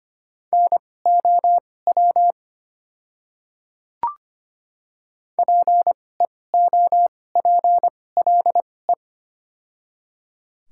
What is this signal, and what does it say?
Morse code, heard as a keyed tone, about 25 wpm, spaced as for 18 wpm, 700 Hz. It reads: NOW PEOPLE
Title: 2x Character Spacing - Top 100 Words - 25wpm